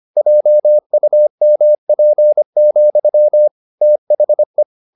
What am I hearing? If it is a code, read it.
Morse code, heard as a keyed tone, about 25 wpm, 600 Hz. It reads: JUMP, THE
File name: the old man and the sea part 1 25wpm